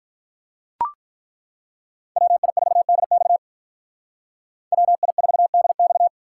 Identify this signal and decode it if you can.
Morse code, heard as a keyed tone, about 42 wpm, 700 Hz. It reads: WI4DX WI4DX
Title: Speed Racing - USA Call Signs 35wpm